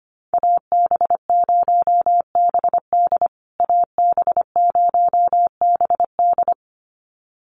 Morse code, heard as a keyed tone, about 25 wpm, 700 Hz. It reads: AB0BD AB0BD
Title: USA Call Signs Repeated 2x 25wpm